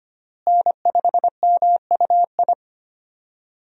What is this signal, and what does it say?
Morse code, heard as a keyed tone, about 25 wpm, 700 Hz. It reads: N5MUI
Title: USA Call Signs - 5 Characters - Repeated 2x - 25wpm